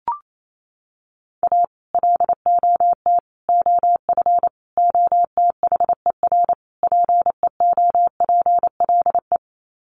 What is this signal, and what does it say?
Morse code, heard as a keyed tone, about 28 wpm, 700 Hz. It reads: A LOT OF OTHER PEOPLE
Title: Sets of 5 Words Encore2 28wpm